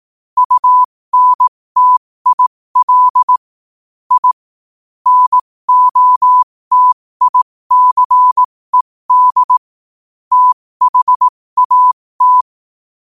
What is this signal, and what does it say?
Morse code, heard as a keyed tone, about 18 wpm, spaced as for 15 wpm, 1000 Hz. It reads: UNTIL I NOTICED THAT
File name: qotdfiles / 15wpm